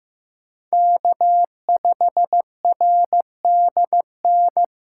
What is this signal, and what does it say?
Morse code, heard as a keyed tone, about 15 wpm, 700 Hz. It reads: K5RDN
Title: USA Callsign DE Callsign 15wpm